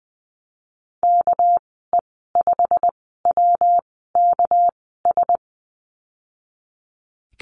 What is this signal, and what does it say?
Morse code, heard as a keyed tone, about 20 wpm, spaced as for 14 wpm, 700 Hz. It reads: KE5WKS
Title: 207 Lesson 33g - Letter j - Call Signs 20wpm